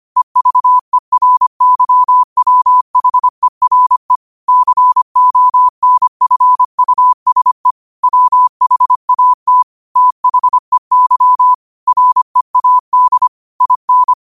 Morse code, heard as a keyed tone, about 25 wpm, 1000 Hz. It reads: EVERYWHERE CONFUSE WHAT THEY READ IN
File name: qotdfiles / 25wpm